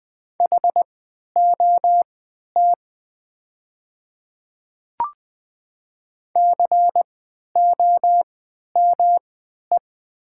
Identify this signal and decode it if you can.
Morse code, heard as a keyed tone, about 20 wpm, spaced as for 11 wpm, 700 Hz. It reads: HOT COME
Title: 3x Character Spacing - Top 100 Words - 20wpm